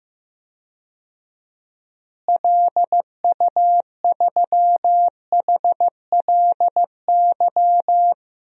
Morse code, heard as a keyed tone, about 15 wpm, 700 Hz. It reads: LU3HLY
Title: DXCC-100 Call Signs - 6 Characters - Repeated 3x - 15wpm